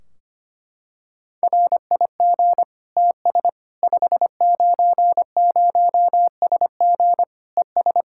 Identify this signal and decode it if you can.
Morse code, heard as a keyed tone, about 25 wpm, 700 Hz. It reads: RIG TS 590SG ES
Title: Practice QSO - Exchange 2 - Info - 25wpm